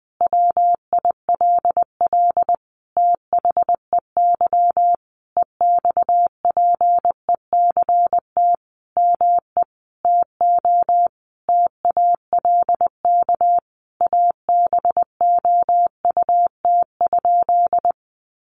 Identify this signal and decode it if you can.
Morse code, heard as a keyed tone, about 20 wpm, 700 Hz. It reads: WILL THEY EXPECT ME TO TALK ABOUT?
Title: Sentences from Top 1100 Words 20wpm